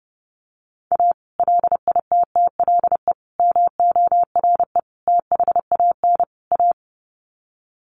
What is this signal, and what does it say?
Morse code, heard as a keyed tone, about 28 wpm, 700 Hz. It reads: A LITTLE MORE THAN A
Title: Sets of 5 Words Encore2 30wpm